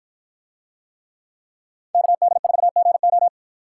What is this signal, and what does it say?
Morse code, heard as a keyed tone, about 49 wpm, 700 Hz. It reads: KD4GO